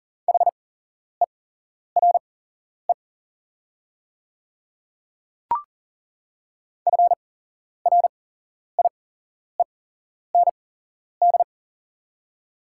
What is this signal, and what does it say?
Morse code, heard as a keyed tone, about 40 wpm, spaced as for 11 wpm, 690 Hz. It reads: HERE FRIEND
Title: 8x Character Spacing - Top 300 Words - 40wpm